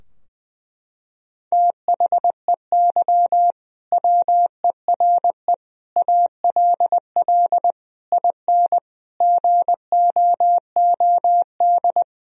Morse code, heard as a keyed tone, about 21 wpm, 700 Hz. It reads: THEY WERE ALL IN GOOD
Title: Sentences from Top 1000 Words - Review Part 3 - 20wpm